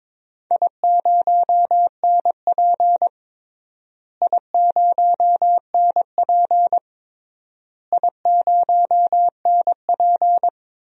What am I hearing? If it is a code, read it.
Morse code, heard as a keyed tone, about 22 wpm, 700 Hz. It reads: I0NP I0NP I0NP
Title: DXCC-100 Call Signs - 4 Characters - Repeated 3x - 22wpm